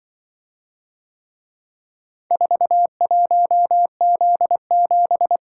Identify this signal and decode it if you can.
Morse code, heard as a keyed tone, about 23 wpm, 700 Hz. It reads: 41Z7
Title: Speed-Racing - Four Letter-Number Combinations - 20wpm